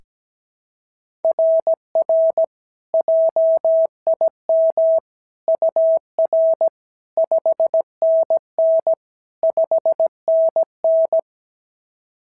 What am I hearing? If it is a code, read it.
Morse code, heard as a keyed tone, about 16 wpm, 650 Hz. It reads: RR JIM UR 5NN 5NN
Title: SOTA QSO Practice 17wpm